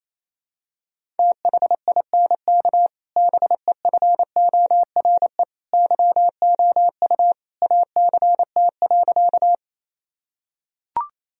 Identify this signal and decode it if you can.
Morse code, heard as a keyed tone, about 28 wpm, 700 Hz. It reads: THINK BEFORE YOU ACT.
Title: Sentences from Top 1000 Words - Review Part 1 - 28wpm